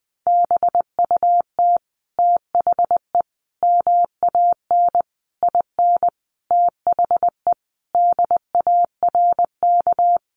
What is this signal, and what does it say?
Morse code, heard as a keyed tone, about 20 wpm, 700 Hz. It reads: BUT THE MAN IN THE DARK
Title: Sentences from Top 600 Words 20wpm